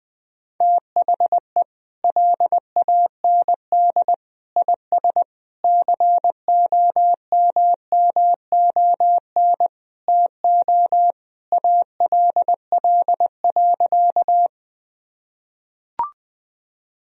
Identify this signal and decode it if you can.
Morse code, heard as a keyed tone, about 20 wpm, 700 Hz. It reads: THE LAND IS COMMON TO ALL.
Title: Sentences from Top 1000 Words - Review Part 3 - 20wpm